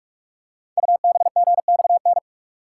Morse code, heard as a keyed tone, about 44 wpm, 700 Hz. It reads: UBGXN
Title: Code Groups - 5 Letters - 45wpm